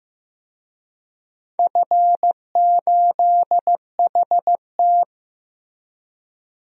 Morse code, heard as a keyed tone, about 15 wpm, 700 Hz. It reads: F8HT
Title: DXCC-100 Call Signs - 4 Characters - Repeated 2x - 15wpm